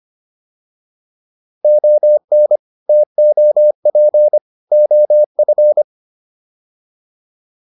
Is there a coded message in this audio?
Yes: Morse code from 1.6 to 5.8 s, heard as a keyed tone, about 25 wpm, 600 Hz. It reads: ON TOP OF